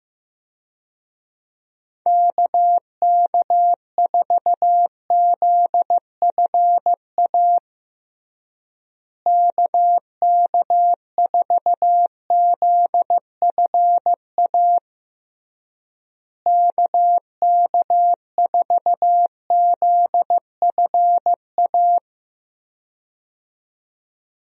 Morse code, heard as a keyed tone, about 15 wpm, 700 Hz. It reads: KK4ZFA KK4ZFA KK4ZFA
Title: USA Call Signs - 6 Characters - Repeated 3x - 15wpm